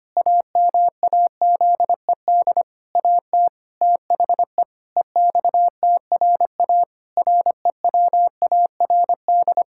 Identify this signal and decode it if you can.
Morse code, heard as a keyed tone, about 25 wpm, 700 Hz. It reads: AMAZED AT THE EXTRA REWARD
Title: Pangram Challenge v2 - 25wpm